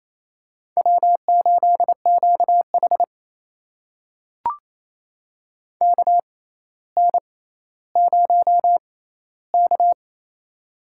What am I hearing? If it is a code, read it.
Morse code, heard as a keyed tone, about 28 wpm, spaced as for 14 wpm, 700 Hz. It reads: W8QH KN0K
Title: USA Call Signs - 4 Characters - 6x Character Spacing - 28wpm